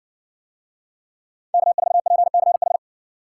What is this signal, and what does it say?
Morse code, heard as a keyed tone, about 60 wpm, 700 Hz. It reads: K4PCH